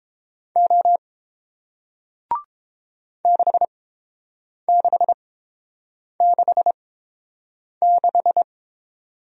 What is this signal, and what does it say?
Morse code, heard as a keyed tone, about 30 wpm, spaced as for 7 wpm, 700 Hz. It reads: O 6666